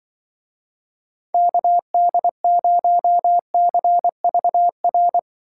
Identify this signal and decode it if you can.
Morse code, heard as a keyed tone, about 24 wpm, 700 Hz. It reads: KD0CVR
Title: Speed Racing - USA 2x3 Call Signs 20wpm